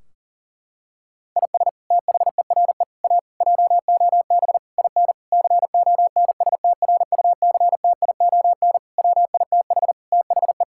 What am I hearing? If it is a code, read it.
Morse code, heard as a keyed tone, about 40 wpm, 700 Hz. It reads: IS THERE A JOB IN CONSTRUCTION WITH THE
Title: Sentences from Top 1600 Words 40wpm